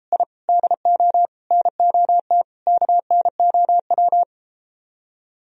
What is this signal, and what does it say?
Morse code, heard as a keyed tone, about 33 wpm, 700 Hz. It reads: I DO NOT KNOW